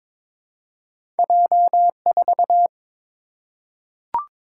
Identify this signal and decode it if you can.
Morse code, heard as a keyed tone, about 22 wpm, 690 Hz. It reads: J4
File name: practice / 22wpm / Two Letter-Number Combinations - 22wpm